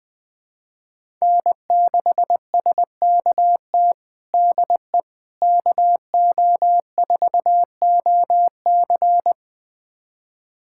Morse code, heard as a keyed tone, about 20 wpm, 700 Hz. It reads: N6SKT DE KO4OC
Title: USA Callsign DE Callsign - 5 Characters - 20wpm